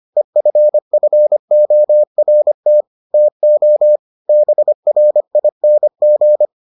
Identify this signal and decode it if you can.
Morse code, heard as a keyed tone, about 25 wpm, 600 Hz. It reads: EFFORT TO BRING